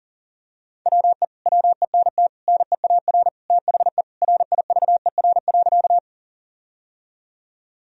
Morse code, heard as a keyed tone, about 40 wpm, 700 Hz. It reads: WE WENT NEAR THE RIVER.
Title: Sentences from Top 1000 Words - Review Part 1 - 40wpm